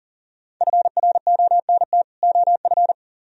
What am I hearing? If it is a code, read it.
Morse code, heard as a keyed tone, about 49 wpm, 700 Hz. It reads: FRONT OF
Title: Speed-Racing - Sets of 2 Words 40wpm